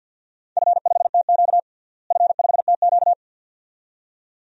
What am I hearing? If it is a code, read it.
Morse code, heard as a keyed tone, about 53 wpm, 700 Hz. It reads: F5TQ F5TQ
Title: DXCC-100 Call Signs - 4 Characters - Repeated 2x - 50wpm